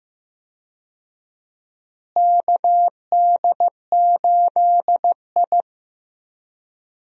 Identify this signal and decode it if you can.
Morse code, heard as a keyed tone, about 15 wpm, 700 Hz. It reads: KD8I
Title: USA Call Signs - 4 Characters - Repeated 2x - 15wpm